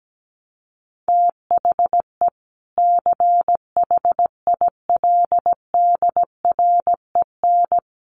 Morse code, heard as a keyed tone, about 17 wpm, 700 Hz. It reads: THE CHILDREN
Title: Sentences from Top 300 Words 17wpm